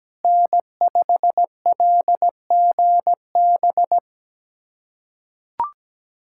Morse code, heard as a keyed tone, about 17 wpm, 700 Hz. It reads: N5LGB
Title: USA Callsign DE Callsign - 5 Characters - 17wpm